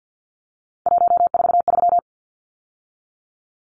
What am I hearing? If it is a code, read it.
Morse code, heard as a keyed tone, about 50 wpm, 700 Hz. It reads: 143
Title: Numbers - 3-Digit 50wpm